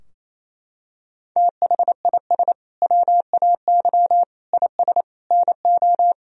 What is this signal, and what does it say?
Morse code, heard as a keyed tone, about 27 wpm, 700 Hz. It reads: THIS WAY IS NO
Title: Sentences from Top 1000 Words - Review Part 2 - 28wpm